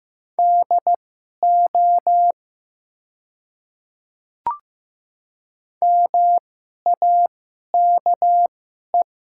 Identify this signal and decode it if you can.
Morse code, heard as a keyed tone, about 15 wpm, spaced as for 11 wpm, 690 Hz. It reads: DO MAKE
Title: 2x Character Spacing - Top 100 Words - 15wpm